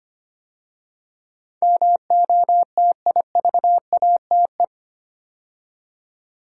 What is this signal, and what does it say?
Morse code, heard as a keyed tone, about 25 wpm, 700 Hz. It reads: MOTIVATE
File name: 8 Letter Words - First 3 Letters Given - 25wpm